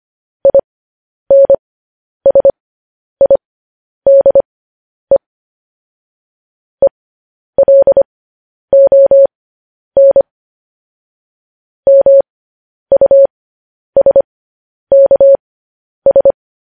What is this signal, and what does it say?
Morse code, heard as a keyed tone, about 25 wpm, spaced as for 10 wpm, 550 Hz. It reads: INSIDE ELON MUSKS
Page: News Headlines in Morse Code at 10 WPM